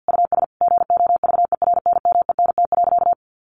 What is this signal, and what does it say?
Morse code, heard as a keyed tone, about 50 wpm, 700 Hz. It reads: US GOVERNMENT.
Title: Easy Phrases 50wpm